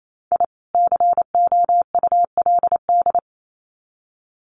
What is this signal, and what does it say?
Morse code, heard as a keyed tone, about 28 wpm, 700 Hz. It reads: I COULD
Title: Sets of 2 Words 28wpm